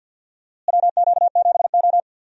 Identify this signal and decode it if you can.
Morse code, heard as a keyed tone, about 52 wpm, 700 Hz. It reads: WQ7O
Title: USA Call Signs - 4 Characters - Repeated 2x - 50wpm